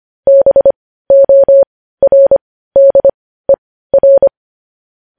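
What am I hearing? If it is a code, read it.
Morse code, heard as a keyed tone, about 25 wpm, spaced as for 15 wpm, 550 Hz. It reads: BORDER
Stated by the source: Morse news headlines sent at 15 words per minute.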